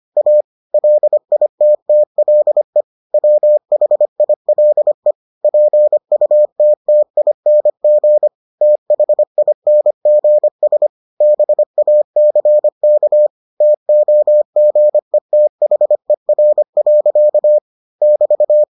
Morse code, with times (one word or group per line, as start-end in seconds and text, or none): 0.2-0.4 A
0.7-2.8 LITTLE
3.1-5.1 WHILE
5.4-8.3 PUTTING
8.6-10.9 THINGS
11.2-13.3 BACK
13.6-17.6 TOGETHER.
18.0-18.6 <BT>